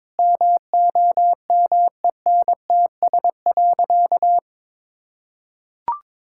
Morse code, heard as a keyed tone, about 21 wpm, 700 Hz. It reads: MOMENTS.
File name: Difficult Sentences with Hints - 22wpm